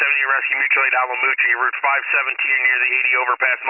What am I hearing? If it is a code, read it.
Emergency Alert System (SAME) tones, heard as harsh buzzing bursts, with speech. It is heard from 0.2 to 2.8 s.